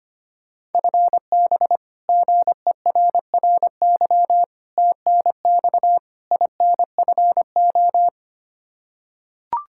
Morse code, heard as a keyed tone, about 25 wpm, 700 Hz. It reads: FB GERRY TNX INFO
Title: Practice QSO - Exchange 3 - More Info - 25wpm